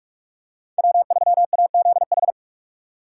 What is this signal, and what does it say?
Morse code, heard as a keyed tone, about 42 wpm, 700 Hz. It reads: W3AZH
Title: USA Call Signs - 5 Characters - 3x Character Spacing - 45wpm